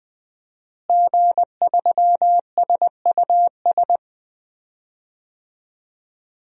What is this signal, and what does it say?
Morse code, heard as a keyed tone, about 20 wpm, 700 Hz. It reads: G3SUS
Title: DXCC-100 Call Signs - 5 Characters - Repeated 2x - 20wpm